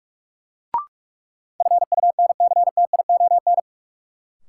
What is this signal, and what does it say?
Morse code, heard as a keyed tone, about 42 wpm, 700 Hz. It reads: FUNCTION
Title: Sentences from Top 1100 Words - New Word Drill Alt - 45wpm